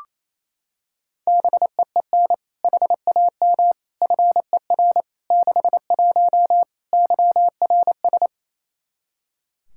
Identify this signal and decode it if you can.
Morse code, heard as a keyed tone, about 27 wpm, 700 Hz. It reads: BEEN HAM FER 61 YRS
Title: K1USN SST QSO Practice 28wpm